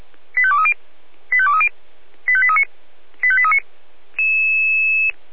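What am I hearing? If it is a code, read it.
Five-tone selective call (ZVEI), heard as quick runs of five tones, tones of about 70 ms. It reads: 85230 85230 85839 85839
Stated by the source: ZVEI five-tone call 85855, received by SDR.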